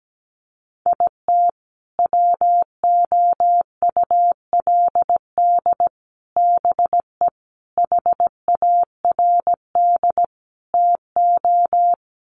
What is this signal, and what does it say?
Morse code, heard as a keyed tone, about 17 wpm, 700 Hz. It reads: IT WOULD BE HARD TO
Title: Sets of 5 Words Encore2 17wpm